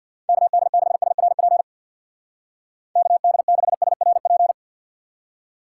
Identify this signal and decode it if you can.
Morse code, heard as a keyed tone, about 65 wpm, 700 Hz. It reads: KD6SRP KD6SRP